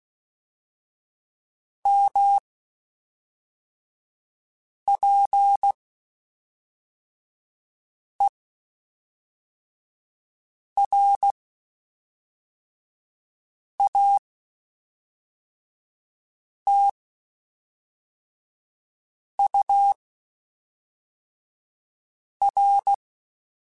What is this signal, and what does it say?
Morse code, heard as a keyed tone, about 16 wpm, spaced as for 3 wpm, 800 Hz. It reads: MPERATUR
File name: Spaced Characters / Spaced Characters-mp3s / Weather5 12 sp @5